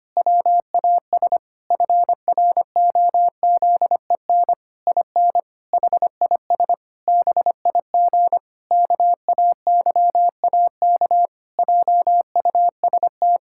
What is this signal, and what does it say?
Morse code, heard as a keyed tone, about 25 wpm, 700 Hz. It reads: WAS FROZEN IN HIS BIG KAYAK JUST
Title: Pangram Challenge v2 - 25wpm